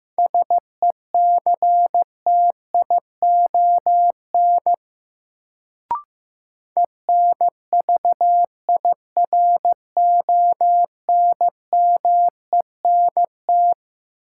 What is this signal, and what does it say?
Morse code, heard as a keyed tone, about 16 wpm, 700 Hz. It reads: SECTION ENVIRONMENT
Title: Sentences from Top 1000 Words - Review Part 5 - New Word Drill - 15wpm